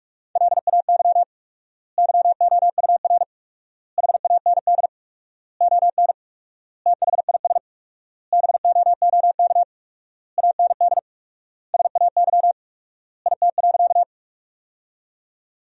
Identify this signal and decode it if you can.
Morse code, heard as a keyed tone, about 46 wpm, 700 Hz. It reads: LAY YOUR HAND ON THIS BOOK AND SAY IT.
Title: 4x Word-Spacing - Sentences from Top 400 Words - 45wpm